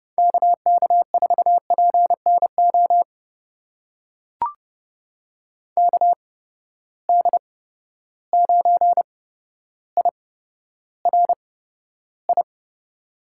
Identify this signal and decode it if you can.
Morse code, heard as a keyed tone, about 30 wpm, spaced as for 13 wpm, 700 Hz. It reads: KK4PNO KD9IRI
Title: USA Call Signs - 6 Characters - 8x Character Spacing - 30wpm